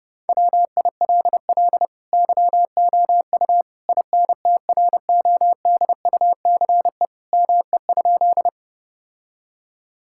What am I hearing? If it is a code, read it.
Morse code, heard as a keyed tone, about 30 wpm, 700 Hz. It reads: WILL YOU INTRODUCE ME?
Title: Sentences from Top 1600 Words 30wpm